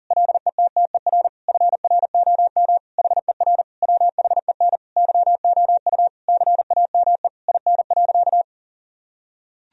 Morse code, heard as a keyed tone, about 40 wpm, 700 Hz. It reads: LETTER FROM HER WHEN YOU CAME IN.
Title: Sentences from Top 1000 Words - Review Part 3 - 40wpm